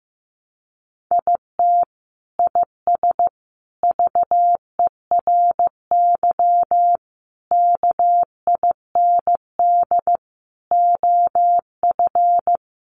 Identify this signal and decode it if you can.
Morse code, heard as a keyed tone, about 15 wpm, 700 Hz. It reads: IT IS VERY KIND OF
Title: Sentences from Top 200 Words 15wpm